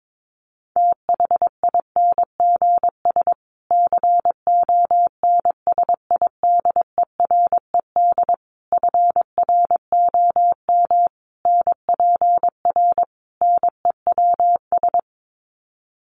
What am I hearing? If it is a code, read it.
Morse code, heard as a keyed tone, about 22 wpm, 700 Hz. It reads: THINGS CONSIDERED FROM NPR NEWS
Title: Sets of 5 Words Encore 22wpm